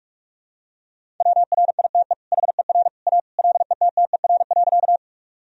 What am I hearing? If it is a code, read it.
Morse code, heard as a keyed tone, about 46 wpm, 700 Hz. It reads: WRITE HER A LETTER.
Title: Sentences from Top 1000 Words - Review Part 1 - 45wpm